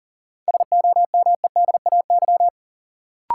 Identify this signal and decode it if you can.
Morse code, heard as a keyed tone, about 40 wpm, 700 Hz. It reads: SOMEDAY
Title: Sentences from Top 1600 Words - New Word Drill Alt - 40wpm